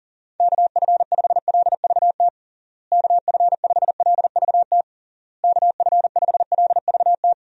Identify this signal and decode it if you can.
Morse code, heard as a keyed tone, about 37 wpm, 700 Hz. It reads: KF5LVT KF5LVT KF5LVT
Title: USA Call Signs - 6 Characters - Repeated 3x - 40wpm